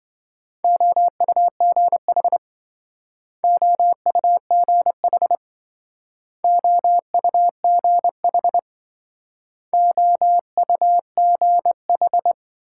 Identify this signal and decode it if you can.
Morse code, heard as a keyed tone, about 26 wpm, 700 Hz. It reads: OUGH OUGH OUGH OUGH